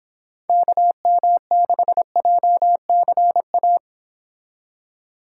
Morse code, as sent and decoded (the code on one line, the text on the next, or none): -.- -- -.... .--- -.-. .-
KM6JCA